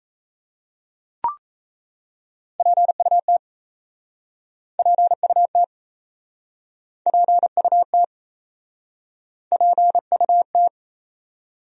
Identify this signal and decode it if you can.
Morse code, heard as a keyed tone, about 37 wpm, 700 Hz. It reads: PUT PUT PUT PUT